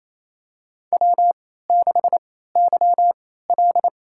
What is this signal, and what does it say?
Morse code, heard as a keyed tone, about 28 wpm, spaced as for 16 wpm, 700 Hz. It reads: W6YL
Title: USA Call Signs - 4 Characters - 3x Character Spacing - 28wpm